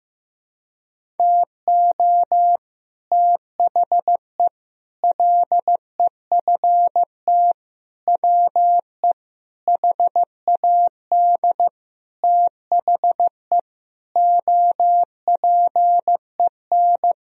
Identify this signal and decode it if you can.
Morse code, heard as a keyed tone, about 15 wpm, 700 Hz. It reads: TO THE LEFT WE HAD THE OPEN
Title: Sentences from Top 1000 Words - Review Part 1 - 15wpm